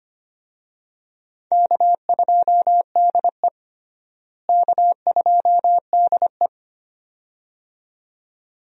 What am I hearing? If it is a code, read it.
Morse code, heard as a keyed tone, about 25 wpm, 700 Hz. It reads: K2DE K2DE
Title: USA Call Signs - 4 Characters - Repeated 2x - 25wpm